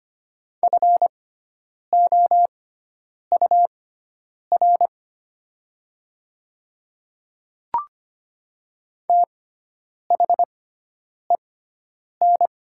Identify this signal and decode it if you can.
Morse code, heard as a keyed tone, about 25 wpm, spaced as for 9 wpm, 700 Hz. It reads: FOUR THEN
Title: 6x Character Spacing - Top 300 Words - 25wpm